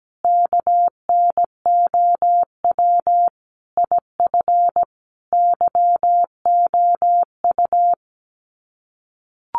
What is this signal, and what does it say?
Morse code, heard as a keyed tone, about 17 wpm, 700 Hz. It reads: KNOW IF YOU
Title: Sets of 5 Words 17wpm